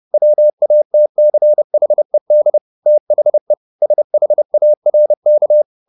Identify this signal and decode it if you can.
Morse code, heard as a keyed tone, about 30 wpm, 600 Hz. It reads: WATCHED THE SHARK